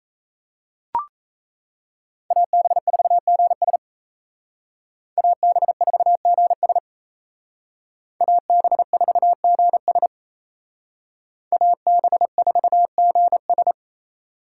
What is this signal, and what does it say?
Morse code, heard as a keyed tone, about 34 wpm, 700 Hz. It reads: AB4GS AB4GS AB4GS AB4GS
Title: Speed Racing - USA 2x2 Call Signs 28wpm